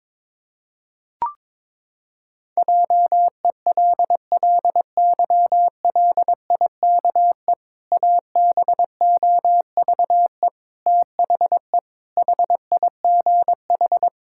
Morse code, heard as a keyed tone, about 22 wpm, 700 Hz. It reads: JELLYLIKE ABOVE THE HIGH
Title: Pangram Challenge v2 - 22wpm